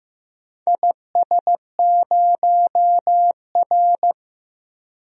Morse code, heard as a keyed tone, about 16 wpm, 700 Hz. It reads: IS0R
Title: DXCC-100 Call Signs - 4 Characters - 15wpm